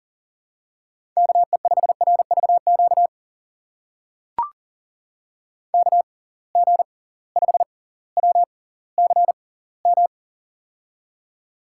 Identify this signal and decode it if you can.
Morse code, heard as a keyed tone, about 40 wpm, spaced as for 20 wpm, 700 Hz. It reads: KE5RVQ KG5WCM
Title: USA Call Signs - 6 Characters - 6x Character Spacing - 40wpm